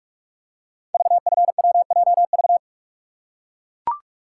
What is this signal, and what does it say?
Morse code, heard as a keyed tone, about 44 wpm, 700 Hz. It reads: VFWJV
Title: Code Groups - 5 Letters - 45wpm